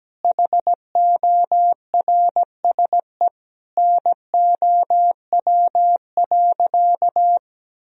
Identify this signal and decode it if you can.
Morse code, heard as a keyed tone, about 17 wpm, 700 Hz. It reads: HORSE NOW.
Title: Sentences from Top 1000 Words - Review Part 1 - 17wpm